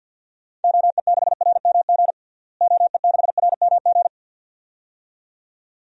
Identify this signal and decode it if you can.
Morse code, heard as a keyed tone, about 50 wpm, 700 Hz. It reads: OE6RMG OE6RMG
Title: DXCC-100 Call Signs - 6 Characters - Repeated 2x - 50wpm